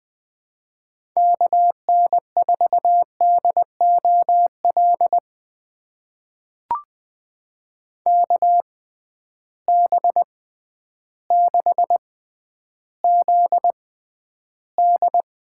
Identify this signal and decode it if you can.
Morse code, heard as a keyed tone, about 20 wpm, spaced as for 11 wpm, 700 Hz. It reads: KN4DOL KB6ZD
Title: USA Call Signs - 6x Character Spacing - 20wpm